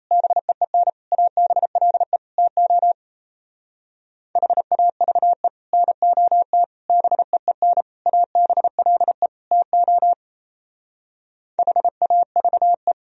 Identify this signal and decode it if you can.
Morse code, heard as a keyed tone, about 34 wpm, 700 Hz. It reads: BEEN ABLE TO HAVE NOT BEEN ABLE TO HAVE
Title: Speed-Racing - Sets of 5 Words 28wpm